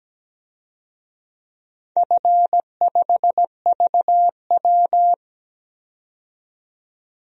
Morse code, heard as a keyed tone, about 17 wpm, 700 Hz. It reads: F5VW
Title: DXCC-100 Call Signs - 4 Characters - 17wpm